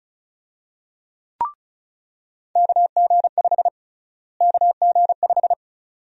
Morse code, heard as a keyed tone, about 36 wpm, 700 Hz. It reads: KG5 KG5
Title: USA Call Sign Prefixes - Repeated 2x - 35wpm